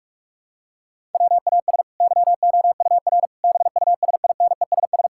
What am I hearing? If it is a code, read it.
Morse code, heard as a keyed tone, about 42 wpm, 700 Hz. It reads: WAS YOUR BUSINESS
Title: Sentences from Top 1000 Words - Review Part 5 - 45wpm